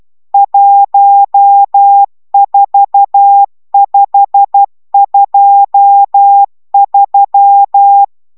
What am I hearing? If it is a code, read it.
Morse code, heard as a keyed tone, about 12 wpm, 800 Hz. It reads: 14523
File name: taviro tanulas / cw 69 22 10